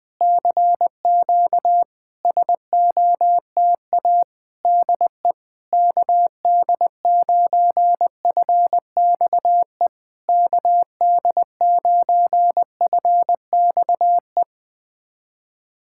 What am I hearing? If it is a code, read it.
Morse code, heard as a keyed tone, about 20 wpm, 700 Hz. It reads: CQ SOTA DE KD9FXE KD9FXE
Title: SOTA QSO Practice 20wpm